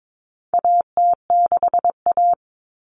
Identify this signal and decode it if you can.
Morse code, heard as a keyed tone, about 22 wpm, 700 Hz. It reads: AT6A
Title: Common Cut Numbers - 4-Digit 22wpm